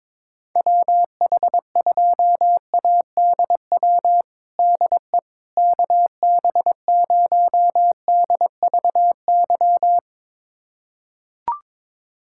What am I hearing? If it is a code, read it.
Morse code, heard as a keyed tone, about 21 wpm, 700 Hz. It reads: WH2ADW DE KB0DVY
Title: USA Callsign DE Callsign - 6 Characters - 22wpm